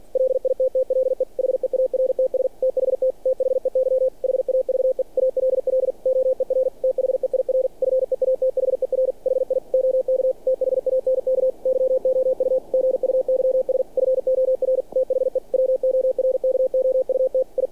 Morse code, heard as a keyed tone, about 48 wpm, 520 Hz. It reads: LITTLE HEARTS THT THEY HAVE ALL OER THEIR FEATHER HI OK THANK YOU GUYS FOR THE WORKOUT I
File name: archived / CRSnet